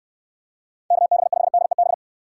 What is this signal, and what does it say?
Morse code, heard as a keyed tone, about 69 wpm, 700 Hz. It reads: KB5DEB